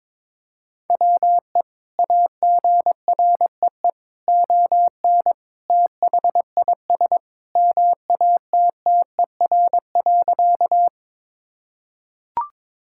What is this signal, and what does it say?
Morse code, heard as a keyed tone, about 22 wpm, 700 Hz. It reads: WE AGREE ON THIS MATTER.
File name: practice / 22wpm / Sentences with 2-Word Phrases - 22wpm